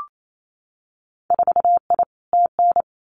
Morse code, heard as a keyed tone, about 28 wpm, 700 Hz. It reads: 4I TN
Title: Winter Field Day Exchanges 28wpm